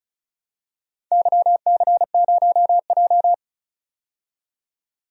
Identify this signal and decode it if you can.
Morse code, heard as a keyed tone, about 34 wpm, 700 Hz. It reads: YC0J
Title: DXCC-100 Call Signs - 4 Characters - 35wpm